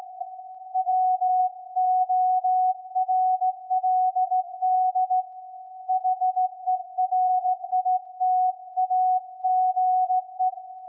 Morse code, heard as a keyed tone, about 17 wpm, 740 Hz. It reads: WORLD HERITAGE
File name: iz7slz / VLF / SAQ 24-12-2021